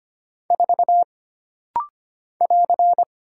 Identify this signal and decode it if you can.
Morse code, heard as a keyed tone, about 25 wpm, 700 Hz. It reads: 4 <AR>
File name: Single Character (Letter:Number:Punctuation:Prosign) - Rapid-Fire - 25wpm